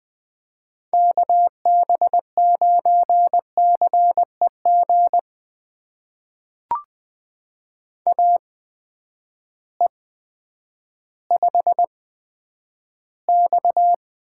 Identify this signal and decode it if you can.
Morse code, heard as a keyed tone, about 20 wpm, spaced as for 10 wpm, 700 Hz. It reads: KB9CEG AE5X
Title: USA Call Signs - 8x Character Spacing - 20wpm